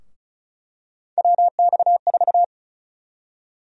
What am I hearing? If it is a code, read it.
Morse code, heard as a keyed tone, about 34 wpm, 700 Hz. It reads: WX4
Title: USA Call Sign Prefixes - Repeated 2x - 35wpm